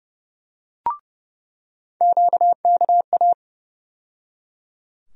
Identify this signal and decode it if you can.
Morse code, heard as a keyed tone, about 30 wpm, 700 Hz. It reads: QKA
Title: USA Call Sign Postfixes - 3 Letters 30wpm